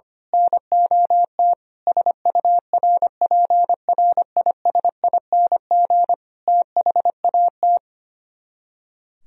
Morse code, heard as a keyed tone, about 24 wpm, 700 Hz. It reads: NOT SURPRISING THAT
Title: Speed-Racing - Sets of 5 Words 25wpm